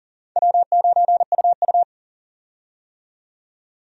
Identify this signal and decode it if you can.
Morse code, heard as a keyed tone, about 40 wpm, 700 Hz. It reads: W9UU
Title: USA Call Signs - 4 Characters - 40wpm